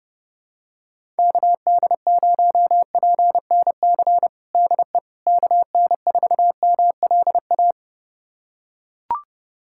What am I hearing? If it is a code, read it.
Morse code, heard as a keyed tone, about 30 wpm, 700 Hz. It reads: KD0PNC DE KN4MLA
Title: USA Callsign DE Callsign - 6 Characters - 30wpm